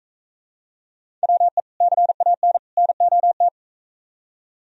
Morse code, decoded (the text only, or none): WE CAN NOT